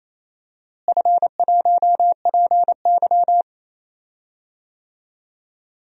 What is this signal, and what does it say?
Morse code, heard as a keyed tone, about 28 wpm, 700 Hz. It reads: F1PY
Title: DXCC-100 Call Signs - 4 Characters - Repeated 3x - 28wpm